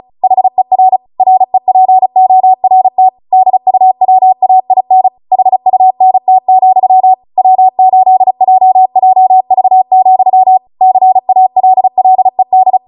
Morse code, heard as a keyed tone, about 36 wpm, 750 Hz. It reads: HER REPORT DUWAIN HUNT, W8JJV, CALLED
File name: Morse / 35WPM